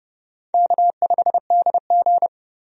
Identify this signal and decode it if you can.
Morse code, heard as a keyed tone, about 30 wpm, 700 Hz. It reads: K5DG
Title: USA Call Signs - 4 Characters - 30wpm